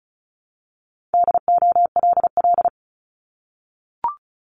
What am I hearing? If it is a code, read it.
Morse code, heard as a keyed tone, about 36 wpm, 700 Hz. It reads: DOLL
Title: Multi-letter Phonemes Repeated 3x 35wpm